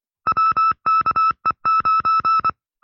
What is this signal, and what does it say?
Morse code, heard as a keyed tone, about 24 wpm, 1310 Hz. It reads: WKE9
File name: FreeFiles / CW-practice / 20WPM-side2